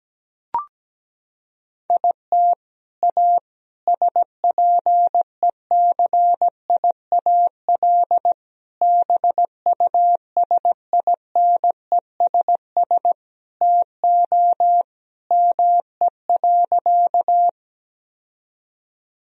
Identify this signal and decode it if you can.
Morse code, heard as a keyed tone, about 17 wpm, 700 Hz. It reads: IT A SPECIAL BUSINESS TO ME.
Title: Sentences from Top 1000 Words - Review Part 3 - 17wpm